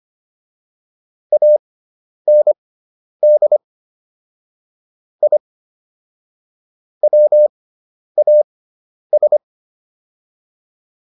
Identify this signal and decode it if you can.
Morse code, heard as a keyed tone, about 25 wpm, spaced as for 10 wpm, 600 Hz. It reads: AND I WAS